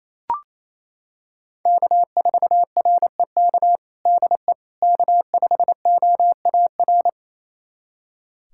Morse code, heard as a keyed tone, about 28 wpm, 700 Hz. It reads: K4REK DE K5OAR
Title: USA Callsign DE Callsign - 5 Characters - 28wpm